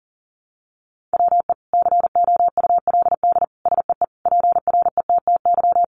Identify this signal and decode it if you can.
Morse code, heard as a keyed tone, about 37 wpm, 700 Hz. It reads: WE COULD SEE PRETTY
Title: Sentences from Top 1000 Words 40wpm